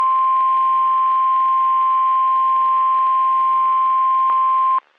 Emergency Alert System (SAME) tones (the 1050 Hz attention tone). It is heard from 0.3 to 4.6 s.
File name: eas messages / EAS WXM31 Elmira